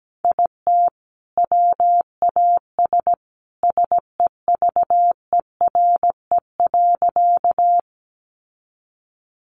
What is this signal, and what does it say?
Morse code, heard as a keyed tone, about 17 wpm, 700 Hz. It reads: IT WAS SEVERE.
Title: Sentences from Top 1400 Words 17wpm